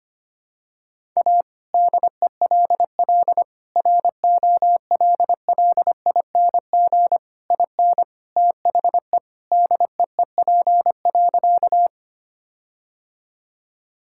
Morse code, heard as a keyed tone, about 25 wpm, 700 Hz. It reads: A DELL ROLLING IN THE DEEP.
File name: practice / 25wpm / All-Age Appropriate Jokes - 25wpm